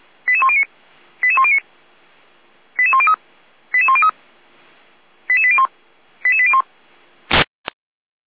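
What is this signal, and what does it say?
Five-tone selective call (ZVEI), heard as quick runs of five tones, tones of about 70 ms. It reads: 88109 88109 88103 88103 88801 88801
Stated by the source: Five-tone ZVEI selective call received on an SDR by feuerwehr-orlamuende.org (Feuerwehr Orlamünde).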